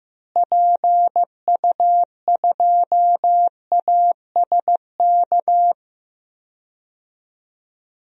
Morse code, heard as a keyed tone, about 15 wpm, 700 Hz. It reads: PU2ASK
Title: DXCC-100 Call Signs - 6 Characters - Repeated 3x - 15wpm